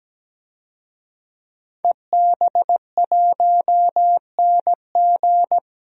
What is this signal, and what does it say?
Morse code, heard as a keyed tone, about 17 wpm, 700 Hz. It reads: EB1NG
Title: DXCC-100 Call Signs - 5 Characters - 17wpm